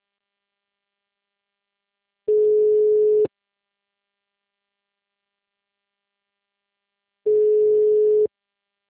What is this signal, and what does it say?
Ringback tone (Europe, 425 Hz).
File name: out- -2030- - - .2434